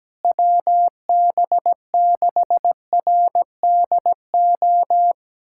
Morse code, heard as a keyed tone, about 17 wpm, 700 Hz. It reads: WB6RDO
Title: USA Callsign DE Callsign 17wpm